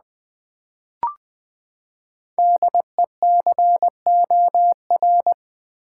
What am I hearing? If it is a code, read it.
Morse code, heard as a keyed tone, about 20 wpm, 700 Hz. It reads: DECOR
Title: Build A Word - 5 Letters At A Time - 10-Letter Words - 20wpm